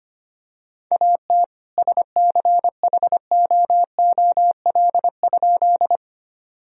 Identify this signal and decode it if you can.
Morse code, heard as a keyed tone, about 25 wpm, 700 Hz. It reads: AT SCHOOL?
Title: Sentences from Top 1000 Words - Review Part 1 - 25wpm